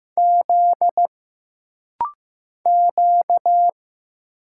Morse code, heard as a keyed tone, about 15 wpm, 700 Hz. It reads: Z Q
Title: Single Letter-Number Alt - Rapid-Fire 15wpm